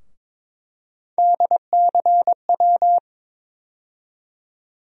Morse code, heard as a keyed tone, about 22 wpm, 700 Hz. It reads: DCW
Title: USA Call Sign Postfixes - 3 Letters 22wpm